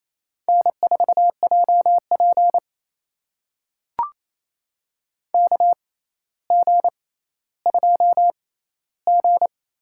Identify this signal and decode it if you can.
Morse code, heard as a keyed tone, about 28 wpm, spaced as for 14 wpm, 700 Hz. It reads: N4JP KG2G
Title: USA Call Signs - 4 Characters - 6x Character Spacing - 28wpm